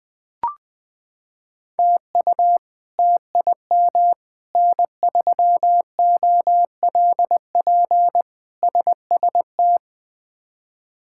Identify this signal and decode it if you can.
Morse code, heard as a keyed tone, about 20 wpm, 700 Hz. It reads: TU TIM N3OLP SST
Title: K1USN SST QSO Practice - 20wpm